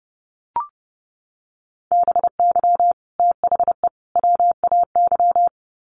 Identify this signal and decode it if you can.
Morse code, heard as a keyed tone, about 30 wpm, 700 Hz. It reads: BY THE WAY
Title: Sets of 3 Words 30wpm